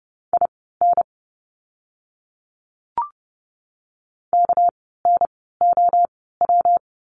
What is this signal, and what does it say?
Morse code, heard as a keyed tone, about 30 wpm, spaced as for 17 wpm, 700 Hz. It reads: IN KNOW
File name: course / 30:10wpm / 237 Lesson 38f - Letter q - Words 30wpm